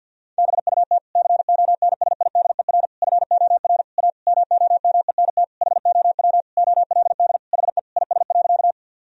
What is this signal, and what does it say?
Morse code, heard as a keyed tone, about 50 wpm, 700 Hz. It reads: BUT CONSIDER FOR A MOMENT HOW OLD HE IS.
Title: Sentences from Top 1000 Words - Review Part 4 - 50wpm